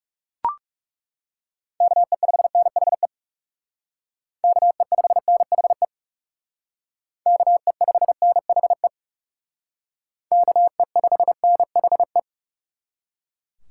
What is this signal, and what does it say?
Morse code, heard as a keyed tone, about 36 wpm, 700 Hz. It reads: KE5NHE KE5NHE KE5NHE KE5NHE
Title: Speed Racing - USA 2x3 Call Signs 30wpm